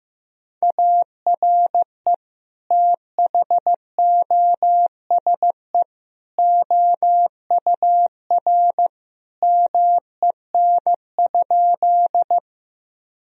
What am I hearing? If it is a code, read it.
Morse code, heard as a keyed tone, about 15 wpm, 700 Hz. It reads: ARE THOSE OUR MEN?
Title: Sentences from Top 1000 Words - Review Part 1 - 15wpm